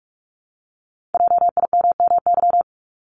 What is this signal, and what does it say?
Morse code, heard as a keyed tone, about 46 wpm, 700 Hz. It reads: JIMMY
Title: Multi-letter Phonemes Repeated 3x 45wpm